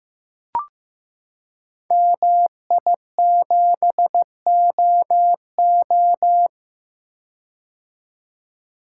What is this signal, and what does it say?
Morse code, heard as a keyed tone, about 15 wpm, 700 Hz. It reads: MI7OO
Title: Code Groups - 5 Letter:Numbers - 15wpm